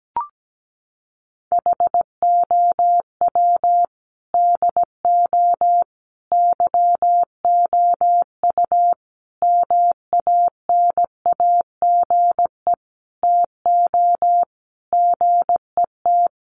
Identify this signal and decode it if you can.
Morse code, heard as a keyed tone, about 17 wpm, 700 Hz. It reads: HOW DO YOU MANAGE TO GET
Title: Sentences from Top 1400 Words 17wpm